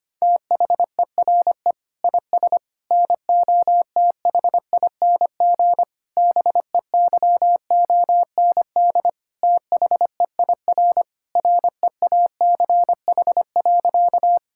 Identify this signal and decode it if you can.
Morse code, heard as a keyed tone, about 25 wpm, 700 Hz. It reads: THERE IS NOTHING BEYOND THEIR REACH.
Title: Sentences from Top 1000 Words - Review Part 6 - 25wpm